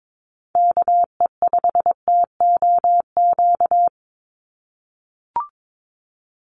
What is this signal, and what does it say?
Morse code, heard as a keyed tone, about 22 wpm, 700 Hz. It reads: KE5TOQ
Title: USA Call Signs 22wpm